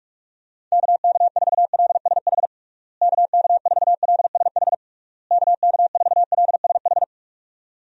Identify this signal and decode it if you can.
Morse code, heard as a keyed tone, about 46 wpm, 700 Hz. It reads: KK4LSH KK4LSH KK4LSH
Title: USA Call Signs - 6 Characters - Repeated 3x - 45wpm